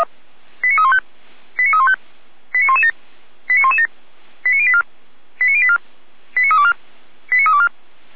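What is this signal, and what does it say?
Five-tone selective call (ZVEI), heard as quick runs of five tones, tones of about 70 ms. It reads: 89315 89315 89117 89117 89084 89084 89324 89324
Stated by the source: ZVEI five-tone selective call received with an SDR, from feuerwehr-orlamuende (Orlamünde fire brigade).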